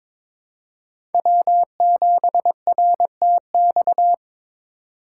Morse code, heard as a keyed tone, about 22 wpm, 700 Hz. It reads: W7RTX
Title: USA Call Signs - 5 Characters - 3x Character Spacing - 22wpm